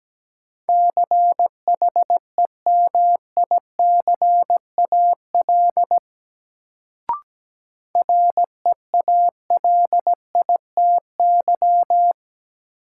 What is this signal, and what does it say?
Morse code, heard as a keyed tone, about 17 wpm, 700 Hz. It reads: CHEMICAL REALITY
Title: Sentences from Top 1200 Words - New Word Drill Alt - 17wpm